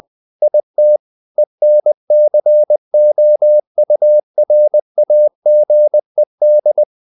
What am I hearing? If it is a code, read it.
Morse code, heard as a keyed tone, about 20 wpm, 600 Hz. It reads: IT ENCOURAGED
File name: the old man and the sea / the old man and the sea part 1 20wpm